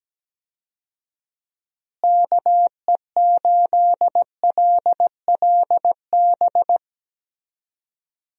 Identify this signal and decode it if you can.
Morse code, heard as a keyed tone, about 17 wpm, 700 Hz. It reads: KE8LLB
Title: USA Call Signs - 6 Characters - Repeated 2x - 17wpm